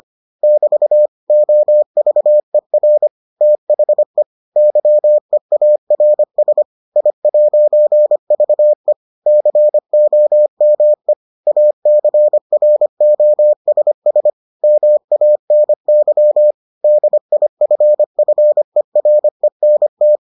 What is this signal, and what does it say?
Morse code, heard as a keyed tone, about 25 wpm, 600 Hz. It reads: <BT> OVER THE YEARS I'VE COME ACROSS MANY DIFFERENT